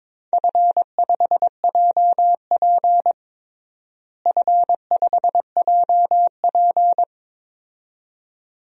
Morse code, heard as a keyed tone, about 22 wpm, 700 Hz. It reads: F5JP F5JP
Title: DXCC-100 Call Signs - 4 Characters - Repeated 2x - 22wpm